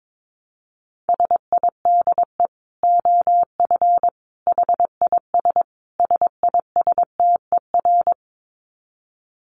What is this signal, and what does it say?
Morse code, heard as a keyed tone, about 22 wpm, 700 Hz. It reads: SIDE OF HIS SISTER
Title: Dit Challenge 22wpm